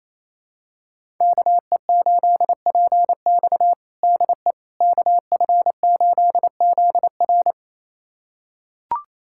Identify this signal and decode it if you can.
Morse code, heard as a keyed tone, about 28 wpm, 700 Hz. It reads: KE8PX DE KF8ZR
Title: USA Callsign DE Callsign - 5 Characters - 28wpm